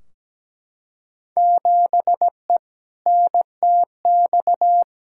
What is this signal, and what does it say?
Morse code, heard as a keyed tone, about 17 wpm, 700 Hz. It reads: 7E NTX
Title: Field Day Exchanges Encore - 17wpm